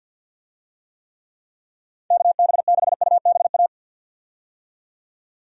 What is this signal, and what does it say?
Morse code, heard as a keyed tone, about 50 wpm, 700 Hz. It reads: KB6UBA
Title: USA Call Signs - 6 Characters - 50wpm